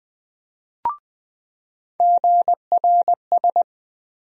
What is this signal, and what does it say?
Morse code, heard as a keyed tone, about 20 wpm, 700 Hz. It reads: GRS
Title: USA Call Sign Postfixes - 3 Letters 20wpm